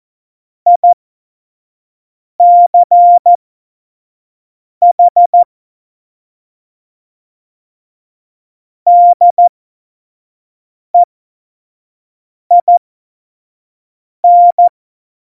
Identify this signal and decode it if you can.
Morse code, heard as a keyed tone, about 14 wpm, spaced as for 5 wpm, 700 Hz. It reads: ICH DEIN
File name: Morsekurs / Witze / Witz 04. Juni 2019, 25 F70